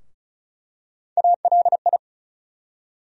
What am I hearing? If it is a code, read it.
Morse code, heard as a keyed tone, about 36 wpm, 700 Hz. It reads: ALI
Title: USA Call Sign Postfixes - 3 Letters 35wpm